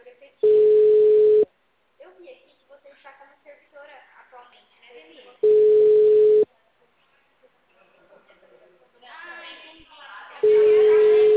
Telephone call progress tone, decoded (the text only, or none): ringback tone